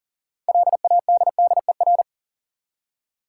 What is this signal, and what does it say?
Morse code, heard as a keyed tone, about 40 wpm, 700 Hz. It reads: LADDER
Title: Six Letter Words - 40wpm